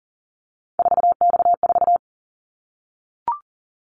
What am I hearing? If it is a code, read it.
Morse code, heard as a keyed tone, about 39 wpm, 700 Hz. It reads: 4X4
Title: Callsign Prefixes for Top 50 DXCC - 40wpm